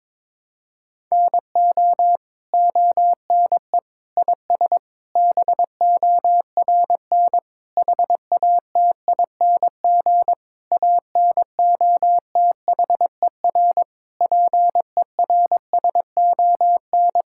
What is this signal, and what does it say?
Morse code, heard as a keyed tone, about 22 wpm, 700 Hz. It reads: NO ONE IS BORN HATING ANOTHER PERSON
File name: Sets of 9 Words - 22wpm